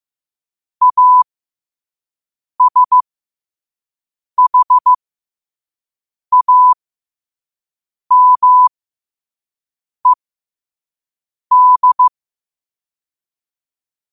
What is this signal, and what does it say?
Morse code, heard as a keyed tone, about 15 wpm, spaced as for 5 wpm, 1000 Hz. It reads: ASHAMED
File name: qotdfiles / 5wpm